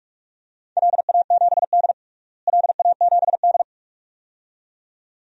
Morse code, heard as a keyed tone, about 45 wpm, 700 Hz. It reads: LA7D LA7D
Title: DXCC-100 Call Signs - 4 Characters - Repeated 2x - 45wpm